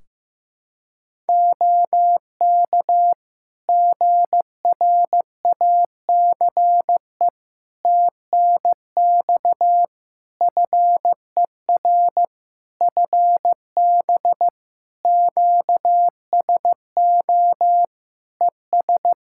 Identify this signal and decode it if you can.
Morse code, heard as a keyed tone, about 15 wpm, 700 Hz. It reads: OK GRACE TNX FER FB QSO ES
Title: Practice QSO - Exchange 4 - Ending - 15wpm